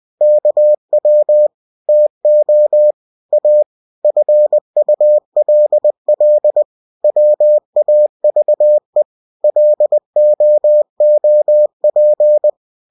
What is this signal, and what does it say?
Morse code, heard as a keyed tone, about 20 wpm, 600 Hz. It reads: KW TO A FULL WAVE LOOP